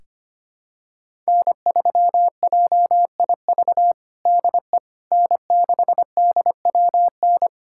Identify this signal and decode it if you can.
Morse code, heard as a keyed tone, about 25 wpm, 700 Hz. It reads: N3JIV DE N6DWN
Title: USA Callsign DE Callsign - 5 Characters - 25wpm